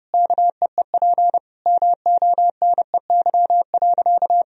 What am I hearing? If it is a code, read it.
Morse code, heard as a keyed tone, about 32 wpm, 700 Hz. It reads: KEEP MONEY.
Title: Sentences from Top 1000 Words - Review Part 4 - 30wpm